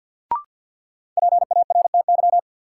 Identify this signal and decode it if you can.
Morse code, heard as a keyed tone, about 49 wpm, 700 Hz. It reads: PARTY
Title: Sentences from Top 1400 Words - New Word Drill Alt - 50wpm